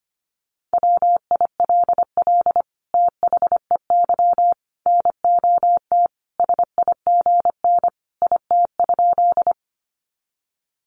Morse code, heard as a keyed tone, about 25 wpm, 700 Hz. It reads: WILL THEY NOT SIGN IT?
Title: Sentences from Top 1000 Words 25wpm